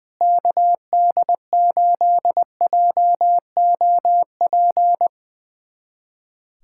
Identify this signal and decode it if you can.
Morse code, heard as a keyed tone, about 21 wpm, 700 Hz. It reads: KD8JOP
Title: USA Callsign DE Callsign - 6 Characters - 20wpm